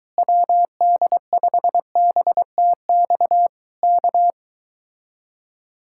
Morse code, heard as a keyed tone, about 23 wpm, 700 Hz. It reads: WD5BTX K